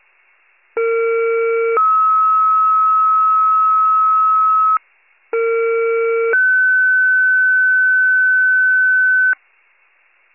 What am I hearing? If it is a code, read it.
Two-tone page, heard as a one-second tone, then a three-second tone, about 460 then 1290 Hz, about 460 then 1500 Hz. It is heard from 0.8 to 9.3 s.